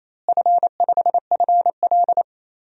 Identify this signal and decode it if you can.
Morse code, heard as a keyed tone, about 28 wpm, 700 Hz. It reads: F5FL
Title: DXCC-100 Call Signs - 4 Characters - Repeated 3x - 28wpm